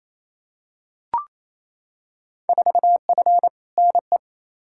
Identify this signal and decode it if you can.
Morse code, heard as a keyed tone, about 28 wpm, 700 Hz. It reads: 4F NE
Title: Field Day Exchanges Encore - 28wpm